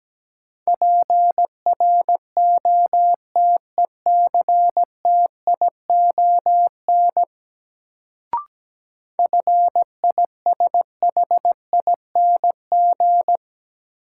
Morse code, heard as a keyed tone, about 17 wpm, 700 Hz. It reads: PROTECTION FISHING
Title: Sentences from Top 1600 Words - New Word Drill Alt - 17wpm